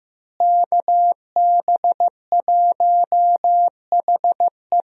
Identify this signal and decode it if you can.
Morse code, heard as a keyed tone, about 15 wpm, 700 Hz. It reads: KB1HE
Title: USA Call Signs - 5 Characters - Repeated 3x - 15wpm